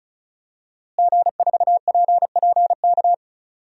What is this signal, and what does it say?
Morse code, heard as a keyed tone, about 36 wpm, 700 Hz. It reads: G4PPK
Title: DXCC-100 Call Signs - 5 Characters - Repeated 2x - 35wpm